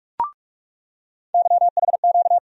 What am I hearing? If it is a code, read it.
Morse code, heard as a keyed tone, about 46 wpm, 700 Hz. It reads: YHQ
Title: USA Call Sign Postfixes - 3 Letters 45wpm